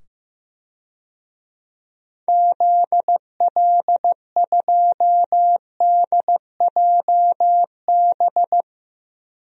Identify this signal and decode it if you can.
Morse code, heard as a keyed tone, about 15 wpm, 700 Hz. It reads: ZL2DJB
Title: DXCC-100 Call Signs - 6 Characters - 15wpm